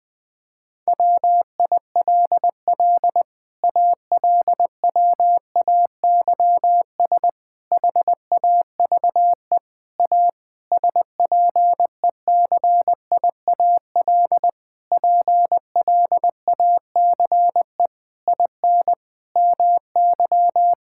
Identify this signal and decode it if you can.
Morse code, heard as a keyed tone, about 20 wpm, 700 Hz. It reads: WILL ALWAYS HAVE A SPECIAL PLACE IN MY
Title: Sets of 9 Words - 20wpm